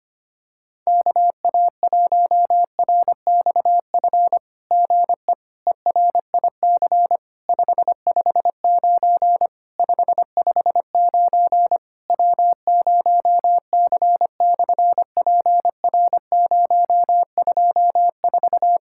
Morse code, heard as a keyed tone, about 25 wpm, 700 Hz. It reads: KA1RXF GE ERIC 559 559 W0C/PR024
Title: SOTA QSO Practice 25wpm